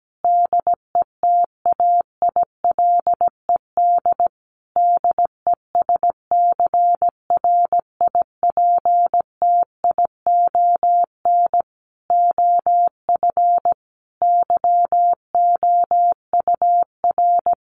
Morse code, heard as a keyed tone, about 17 wpm, 700 Hz. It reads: DETAILED DESCRIPTION OF YOUR
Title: Sentences from Top 1500 Words 17wpm